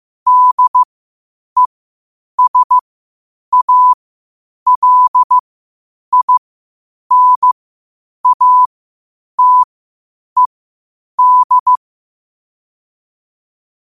Morse code, heard as a keyed tone, about 15 wpm, spaced as for 8 wpm, 1000 Hz. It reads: DESALINATED
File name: qotdfiles / 8wpm